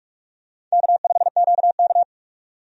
Morse code, heard as a keyed tone, about 46 wpm, 700 Hz. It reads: K5QK
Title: USA Call Signs - 4 Characters - 45wpm